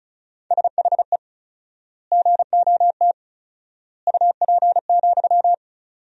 Morse code, heard as a keyed tone, about 34 wpm, 700 Hz. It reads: SHE GOT UP,
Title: 4x Word-Spacing - Sentences from Top 500 Words - 35wpm